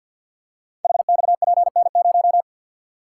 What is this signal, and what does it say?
Morse code, heard as a keyed tone, about 48 wpm, 700 Hz. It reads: HXPN0
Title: Code Groups - 5 Letter:Numbers - 50wpm